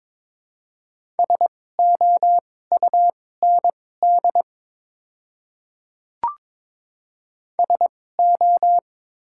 Morse code, heard as a keyed tone, about 21 wpm, spaced as for 16 wpm, 700 Hz. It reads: SOUND SO
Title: 2x Character Spacing - Top 100 Words - 22wpm